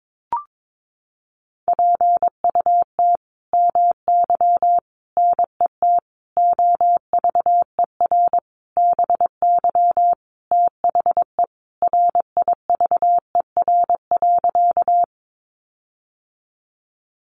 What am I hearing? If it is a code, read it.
Morse code, heard as a keyed tone, about 22 wpm, 700 Hz. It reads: PUT MY NET OVER BY THE RIVER.
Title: Sentences from Top 1400 Words 22wpm